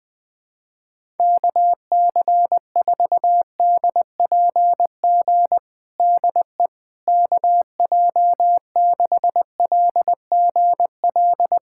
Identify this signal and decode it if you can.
Morse code, heard as a keyed tone, about 20 wpm, 700 Hz. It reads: KC4DPG DE KJ6LGL
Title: USA Callsign DE Callsign - 6 Characters - 20wpm